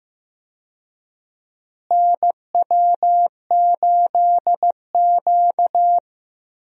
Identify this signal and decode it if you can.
Morse code, heard as a keyed tone, about 15 wpm, 700 Hz. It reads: NW8Q
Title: USA Call Signs - 4 Characters - Repeated 3x - 15wpm